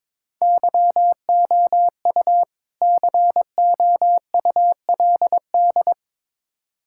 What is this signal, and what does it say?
Morse code, heard as a keyed tone, about 22 wpm, 700 Hz. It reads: YOU COULD
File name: Build A Phrase - 2-Word Phrases - 22wpm